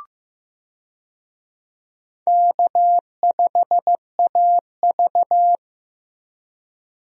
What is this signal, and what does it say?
Morse code, heard as a keyed tone, about 15 wpm, 700 Hz. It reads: K5AV
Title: USA Call Signs - 4 Characters - Repeated 3x - 15wpm